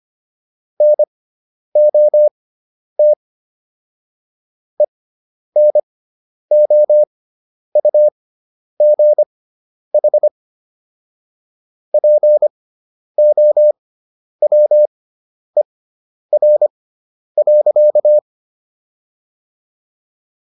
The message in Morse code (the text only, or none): NOT ENOUGH POWER.